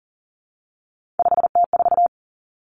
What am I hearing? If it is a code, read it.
Morse code, heard as a keyed tone, about 42 wpm, 700 Hz. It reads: 5T4
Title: Three Challenging Letter-Number Combinations 40wpm